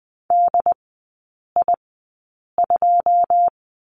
Morse code, heard as a keyed tone, about 20 wpm, spaced as for 8 wpm, 690 Hz. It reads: DI2
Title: 081 Lesson 14e - Number 2 - Three Characters 20wpm